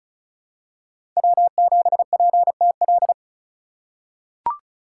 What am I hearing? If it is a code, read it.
Morse code, heard as a keyed tone, about 36 wpm, 700 Hz. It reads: W7PTL
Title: USA Call Signs - 5 Characters - 2x Character Spacing - 35wpm